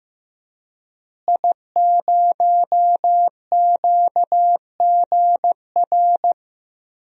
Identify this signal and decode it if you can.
Morse code, heard as a keyed tone, about 15 wpm, 700 Hz. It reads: I0QGR
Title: DXCC-100 Call Signs - 5 Characters - Repeated 2x - 15wpm